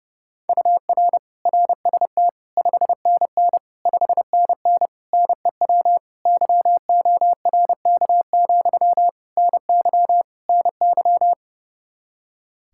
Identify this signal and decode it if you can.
Morse code, heard as a keyed tone, about 32 wpm, 700 Hz. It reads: UR RST 5NN 5NN NEW YORK, NY NY
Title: Practice QSO - Exchange 1 - Beginning - 30wpm